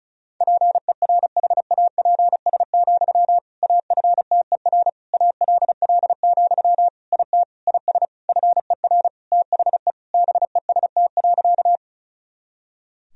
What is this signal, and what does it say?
Morse code, heard as a keyed tone, about 34 wpm, 700 Hz. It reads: PERHAPS, AFTER ALL, IT IS FER THE BEST.
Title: Sentences from Top 500 Words and Abbreviations 35wpm